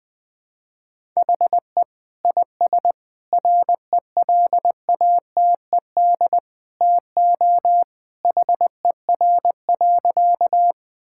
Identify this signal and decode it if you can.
Morse code, heard as a keyed tone, about 21 wpm, 700 Hz. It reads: HE IS RELATED TO HER.
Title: Sentences from Top 1000 Words - Review Part 5 - 20wpm